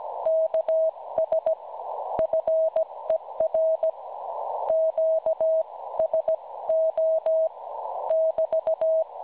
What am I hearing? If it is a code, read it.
Morse code, heard as a keyed tone, about 17 wpm, spaced as for 15 wpm, 640 Hz. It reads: KS FER QSO <BT>
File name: cwronde / websdr recording .6kHz